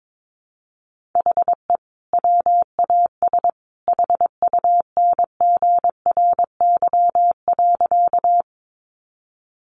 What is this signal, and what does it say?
Morse code, heard as a keyed tone, about 22 wpm, 700 Hz. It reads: HE WAS HUNGRY.
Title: Sentences from Top 1400 Words 22wpm